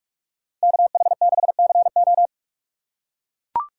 Morse code, heard as a keyed tone, about 43 wpm, 700 Hz. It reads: KH6CO